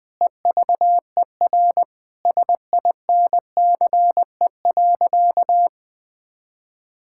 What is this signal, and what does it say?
Morse code, heard as a keyed tone, about 20 wpm, 700 Hz. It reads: EVER SINCE.
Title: Sentences from Top 1000 Words - Review Part 5 - 20wpm